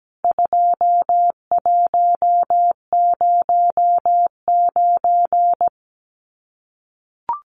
Morse code, heard as a keyed tone, about 17 wpm, 700 Hz. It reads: 2109
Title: CWT Exchange 17wpm